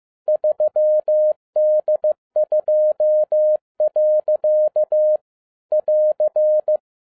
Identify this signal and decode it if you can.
Morse code, heard as a keyed tone, about 15 wpm, 610 Hz. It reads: 3D2. <AR>
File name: 2011-10-mp3 / n07-A DOT 15wpm.mp3